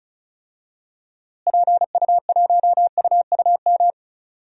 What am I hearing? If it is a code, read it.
Morse code, heard as a keyed tone, about 34 wpm, 700 Hz. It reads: PU1UUM
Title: DXCC-100 Call Signs - 6 Characters - Repeated 2x - 35wpm